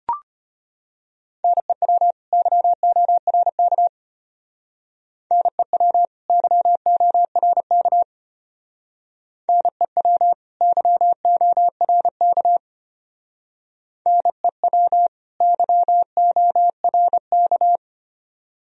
Morse code, heard as a keyed tone, about 33 wpm, 700 Hz. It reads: NEW YORK NEW YORK NEW YORK NEW YORK